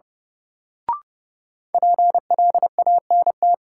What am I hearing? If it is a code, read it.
Morse code, heard as a keyed tone, about 30 wpm, 700 Hz. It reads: PLANT
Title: Sentences from Top 1000 Words - Review Part 1 - New Word Drill - 30wpm